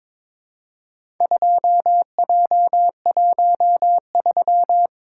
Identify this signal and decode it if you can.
Morse code, heard as a keyed tone, about 22 wpm, 700 Hz. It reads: 2J13
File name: practice / 22wpm / Four Letter-Number Combinations - 22wpm